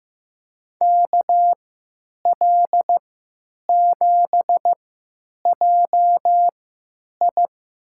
Morse code, heard as a keyed tone, about 15 wpm, spaced as for 9 wpm, 700 Hz. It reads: KL7JI
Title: USA Call Signs - 3x Character Spacing - 15wpm